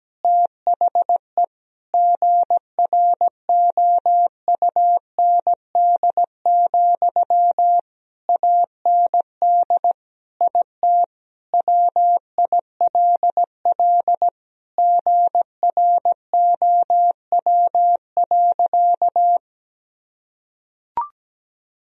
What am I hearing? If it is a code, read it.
Morse code, heard as a keyed tone, about 17 wpm, 700 Hz. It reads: THE GROUND, AND IT WILL GROW.
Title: Sentences from Top 1000 Words - Review Part 2 - 17wpm